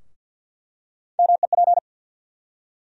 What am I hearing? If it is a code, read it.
Morse code, heard as a keyed tone, about 47 wpm, 700 Hz. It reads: MEP
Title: USA Call Sign Postfixes - 3 Letters 50wpm